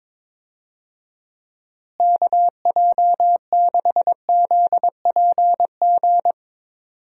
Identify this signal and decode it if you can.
Morse code, heard as a keyed tone, about 21 wpm, 700 Hz. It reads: KJ6ZPG
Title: USA Call Signs - 6 Characters - Repeated 2x - 22wpm